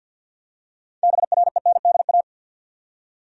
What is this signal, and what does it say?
Morse code, heard as a keyed tone, about 52 wpm, 700 Hz. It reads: BRENDA
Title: USA Common Names - 50wpm